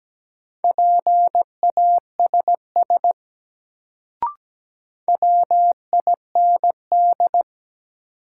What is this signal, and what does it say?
Morse code, heard as a keyed tone, about 17 wpm, 700 Hz. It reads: PASS WIND
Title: Sentences from Top 400 Words - New Word Drill - 17wpm